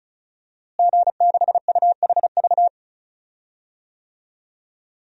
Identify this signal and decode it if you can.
Morse code, heard as a keyed tone, about 36 wpm, 700 Hz. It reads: G6UHV
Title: DXCC-100 Call Signs - 5 Characters - Repeated 3x - 35wpm